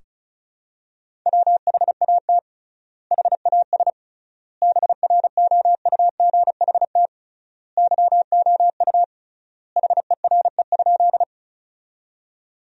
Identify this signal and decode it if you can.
Morse code, heard as a keyed tone, about 34 wpm, 700 Hz. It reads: WHAT HAS BROUGHT YOU HERE?
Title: 3x Word-Spacing - Sentences from Top 500 Words - 35wpm